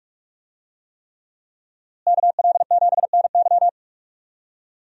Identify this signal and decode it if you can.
Morse code, heard as a keyed tone, about 44 wpm, 700 Hz. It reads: KL7NY
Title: USA Call Signs - 5 Characters - Repeated 3x - 45wpm